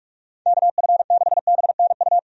Morse code, heard as a keyed tone, about 46 wpm, 700 Hz. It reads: KF6BNU
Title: USA Call Signs - 6 Characters - 45wpm